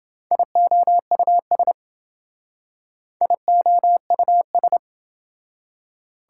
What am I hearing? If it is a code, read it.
Morse code, heard as a keyed tone, about 27 wpm, 700 Hz. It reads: IOUS IOUS